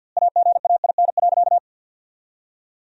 Morse code, heard as a keyed tone, about 50 wpm, 700 Hz. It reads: AGAIN.
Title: 2x Word-Spacing - Sentences from Top 500 Words - 50wpm